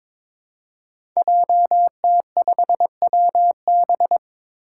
Morse code, heard as a keyed tone, about 22 wpm, 700 Hz. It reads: JT5WB
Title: Five Challenging Letter-Number Combinations 22wpm